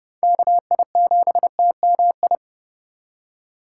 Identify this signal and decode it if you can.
Morse code, heard as a keyed tone, about 30 wpm, 700 Hz. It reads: KI7TMI
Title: USA Call Signs - 6 Characters - 3x Character Spacing - 30wpm